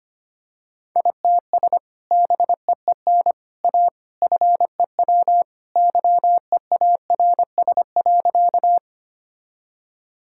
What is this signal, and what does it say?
Morse code, heard as a keyed tone, about 25 wpm, 700 Hz. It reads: ITS BEEN A FEW YEARS.
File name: practice / 25wpm / Sentences with 3-Word Phrases - 25wpm